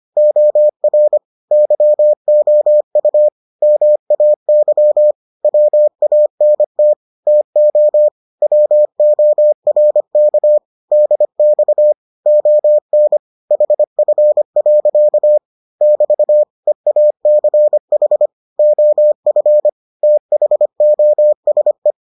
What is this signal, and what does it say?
Morse code, heard as a keyed tone, about 25 wpm, 600 Hz. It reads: OR YOU MAY WANT TO WORK DX ON HF. <BT> EACH OF THOSE